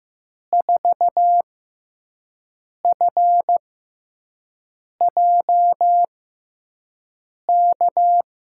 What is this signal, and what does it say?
Morse code, heard as a keyed tone, about 15 wpm, spaced as for 5 wpm, 700 Hz. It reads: 4FJK
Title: USA Call Signs - 6 Characters - 6x Character Spacing - 15wpm